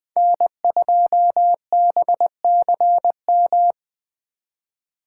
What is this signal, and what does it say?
Morse code, heard as a keyed tone, about 24 wpm, 700 Hz. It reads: N2BCM
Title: Speed Racing - USA 1x3 Call Signs 20wpm